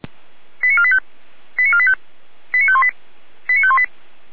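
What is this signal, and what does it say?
Five-tone selective call (ZVEI), heard as quick runs of five tones, tones of about 70 ms. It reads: 89485 89485 89518 89518
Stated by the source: Received on an SDR.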